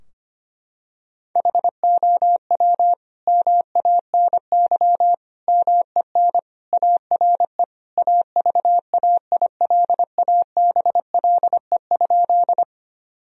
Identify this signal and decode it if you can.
Morse code, heard as a keyed tone, about 25 wpm, 700 Hz. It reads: HOW MANY MEN ARE AVAILABLE?
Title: Sentences from Top 1000 Words - Review Part 3 - 25wpm